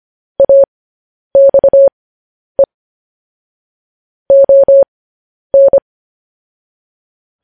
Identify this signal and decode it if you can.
Morse code, heard as a keyed tone, about 25 wpm, spaced as for 10 wpm, 550 Hz. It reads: AXE ON
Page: news-headlines-in-morse-code-at-10-wpm